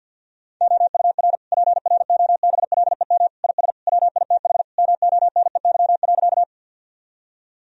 Morse code, heard as a keyed tone, about 50 wpm, 700 Hz. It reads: OUR PROBLEM IS WITH MONEY.
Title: Sentences from Top 1000 Words - Review Part 2 - 50wpm